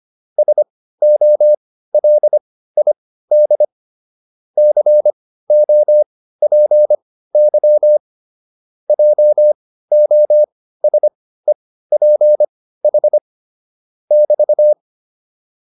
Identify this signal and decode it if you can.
Morse code, heard as a keyed tone, about 25 wpm, spaced as for 15 wpm, 600 Hz. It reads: SOLID COPY JOSEPH <BT>
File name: QSO 302 15